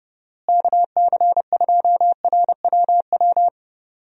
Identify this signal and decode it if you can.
Morse code, heard as a keyed tone, about 31 wpm, 700 Hz. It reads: KC2RWW
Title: USA Call Signs - 6 Characters - Repeated 2x - 30wpm